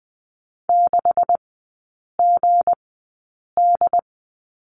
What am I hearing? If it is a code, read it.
Morse code, heard as a keyed tone, about 20 wpm, spaced as for 8 wpm, 700 Hz. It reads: 6GD
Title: 159 Lesson 26e - Letter g - Three Characters 20wpm